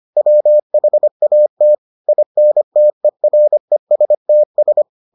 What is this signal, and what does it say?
Morse code, heard as a keyed tone, about 25 wpm, 600 Hz. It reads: WHAT INTERESTS